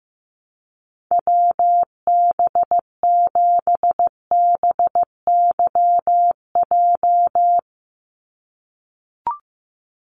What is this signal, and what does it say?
Morse code, heard as a keyed tone, about 15 wpm, 700 Hz. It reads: WB7BYJ
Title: USA Call Signs 15wpm